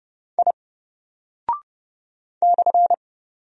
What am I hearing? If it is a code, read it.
Morse code, heard as a keyed tone, about 30 wpm, 700 Hz. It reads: I /
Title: Single Character (Letter:Number:Punctuation:Prosign) Alt 30wpm